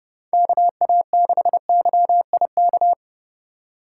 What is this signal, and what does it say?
Morse code, heard as a keyed tone, about 32 wpm, 700 Hz. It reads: KA6YIK